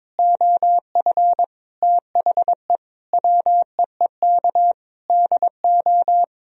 Morse code, heard as a keyed tone, about 21 wpm, 700 Hz. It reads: OF THE WEEK DO
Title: Sentences from Top 1000 Words - Review Part 3 - 22wpm